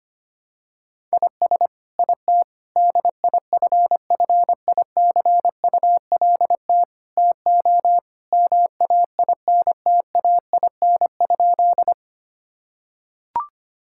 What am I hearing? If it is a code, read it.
Morse code, heard as a keyed tone, about 25 wpm, 700 Hz. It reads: IS IT DIFFICULT TO MAINTAIN?
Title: Sentences from Top 1000 Words - Review Part 5 - 25wpm